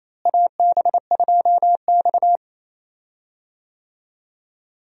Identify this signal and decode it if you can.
Morse code, heard as a keyed tone, about 27 wpm, 700 Hz. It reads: AB2X
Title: USA Call Signs - 4 Characters - Repeated 2x - 28wpm